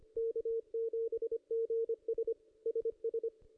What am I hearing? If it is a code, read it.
Morse code, heard as a keyed tone, about 25 wpm, 470 Hz. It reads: K7GS SS
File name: contests / ss-cw-k5tr-op-n5rz -1800